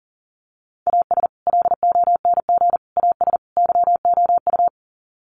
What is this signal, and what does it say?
Morse code, heard as a keyed tone, about 40 wpm, 700 Hz. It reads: AS LONG AS YOU
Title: Sets of 4 Words 40wpm